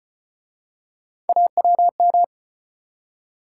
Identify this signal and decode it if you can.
Morse code, heard as a keyed tone, about 38 wpm, 700 Hz. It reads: AWM